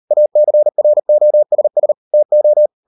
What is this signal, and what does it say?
Morse code, heard as a keyed tone, about 38 wpm, 600 Hz. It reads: ACROSS TO